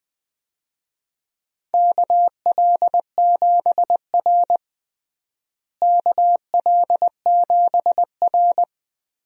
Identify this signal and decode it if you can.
Morse code, heard as a keyed tone, about 20 wpm, 700 Hz. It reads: KL7R KL7R
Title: DXCC-100 Call Signs - 4 Characters - Repeated 3x - 20wpm